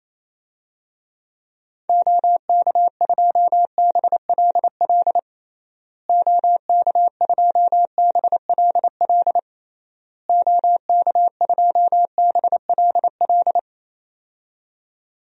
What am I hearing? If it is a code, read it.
Morse code, heard as a keyed tone, about 27 wpm, 700 Hz. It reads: OK2BLL OK2BLL OK2BLL
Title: DXCC-100 Call Signs - 6 Characters - Repeated 3x - 28wpm